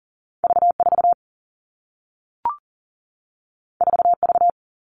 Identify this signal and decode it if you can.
Morse code, heard as a keyed tone, about 42 wpm, 700 Hz. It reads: V4 4V
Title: Two Challenging Letter-Number Combinations 40wpm